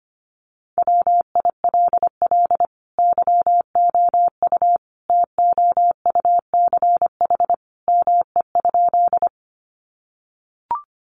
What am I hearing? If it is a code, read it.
Morse code, heard as a keyed tone, about 25 wpm, 700 Hz. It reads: WILL YOU TOUCH ME?
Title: Sentences from Top 1200 Words 25wpm